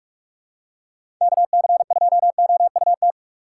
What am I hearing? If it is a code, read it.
Morse code, heard as a keyed tone, about 46 wpm, 700 Hz. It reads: KC2OUT
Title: USA Call Signs - 6 Characters - 6x Character Spacing - 45wpm